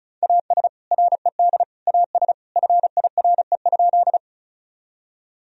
Morse code, heard as a keyed tone, about 36 wpm, 700 Hz. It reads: AS RED AS FIRE?
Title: Sentences from Top 1000 Words - Review Part 2 - 35wpm